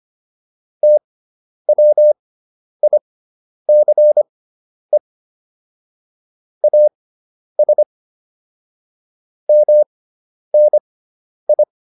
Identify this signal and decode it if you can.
Morse code, heard as a keyed tone, about 25 wpm, spaced as for 10 wpm, 600 Hz. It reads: TWICE AS MNI